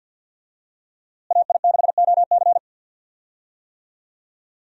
Morse code, heard as a keyed tone, about 52 wpm, 700 Hz. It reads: AI6OC
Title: USA Call Signs - 5 Characters - Repeated 2x - 50wpm